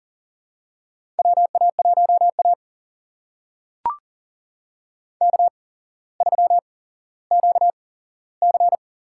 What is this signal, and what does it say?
Morse code, heard as a keyed tone, about 42 wpm, spaced as for 17 wpm, 700 Hz. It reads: WA1A K3QC
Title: USA Call Signs - 4 Characters - 8x Character Spacing - 40wpm